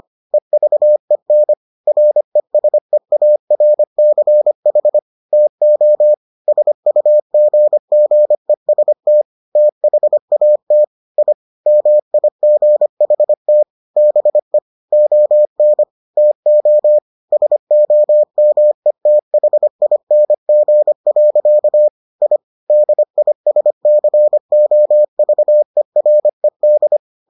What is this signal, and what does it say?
Morse code, heard as a keyed tone, about 25 wpm, 600 Hz. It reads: EVEN RESEARCH TO SUGGEST THAT I MIGHT BE ON TO SOMETHING. I DISCOVERED